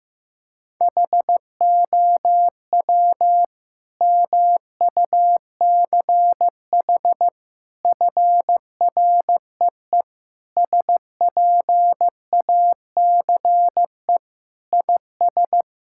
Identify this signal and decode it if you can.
Morse code, heard as a keyed tone, about 16 wpm, 700 Hz. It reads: HOW MUCH FREE SPACE IS
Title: Sentences from Top 1000 Words - Review Part 6 - 15wpm